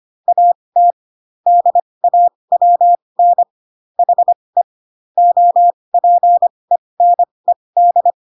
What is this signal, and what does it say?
Morse code, heard as a keyed tone, about 25 wpm, spaced as for 20 wpm, 700 Hz. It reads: AT DAWN HE OPENED
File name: Pinocchio / Pinocchio ch6 15 25 ik1hge